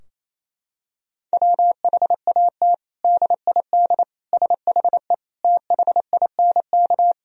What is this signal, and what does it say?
Morse code, heard as a keyed tone, about 28 wpm, 700 Hz. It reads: WHAT DID SHE THINK
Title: Sentences from Top 1000 Words - Review Part 4 - 28wpm